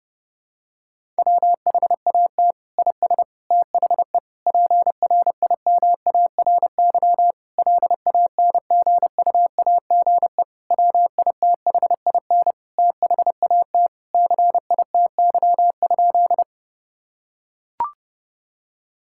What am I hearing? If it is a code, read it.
Morse code, heard as a keyed tone, about 32 wpm, 700 Hz. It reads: WHAT IS THE PRIMARY LANGUAGE WITHIN THAT CITY?
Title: Sentences from Top 1000 Words - Review Part 5 - 30wpm